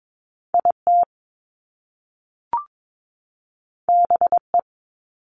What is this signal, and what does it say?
Morse code, heard as a keyed tone, about 22 wpm, 700 Hz. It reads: IT BE
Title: Top Two Letter Combinations in Words 22wpm